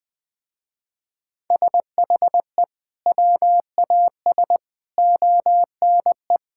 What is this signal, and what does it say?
Morse code, heard as a keyed tone, about 20 wpm, 700 Hz. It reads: SHE WAS ONE
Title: Sentences with 4-Word Phrases - Spoken Beginning - 20wpm